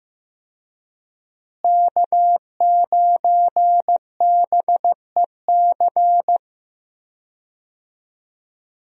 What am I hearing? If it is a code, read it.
Morse code, heard as a keyed tone, about 16 wpm, 700 Hz. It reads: K9BEC
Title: USA Call Signs - 5 Characters - Repeated 3x - 15wpm